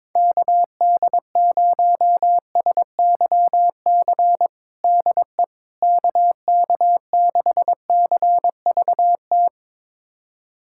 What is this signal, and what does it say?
Morse code, heard as a keyed tone, about 22 wpm, 700 Hz. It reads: KD0SYC DE KK6CVT
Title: USA Callsign DE Callsign 22wpm